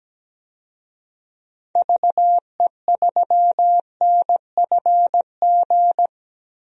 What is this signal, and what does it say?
Morse code, heard as a keyed tone, about 17 wpm, 700 Hz. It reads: VE3NFG
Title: DXCC-100 Call Signs - 6 Characters - Repeated 3x - 17wpm